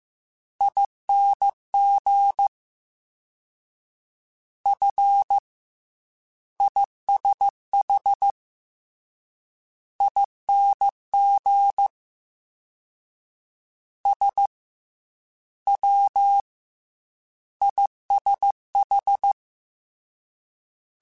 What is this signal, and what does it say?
Morse code, heard as a keyed tone, about 15 wpm, spaced as for 9 wpm, 800 Hz. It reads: ING FISHING SWISH